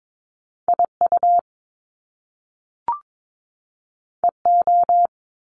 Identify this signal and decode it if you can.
Morse code, heard as a keyed tone, about 22 wpm, 700 Hz. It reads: IU EO
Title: Top Two Letter Combinations in Words 22wpm